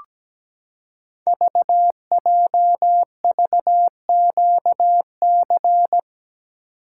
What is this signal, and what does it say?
Morse code, heard as a keyed tone, about 17 wpm, 700 Hz. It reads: VJVQC
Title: Code Groups - 5 Letters - 17wpm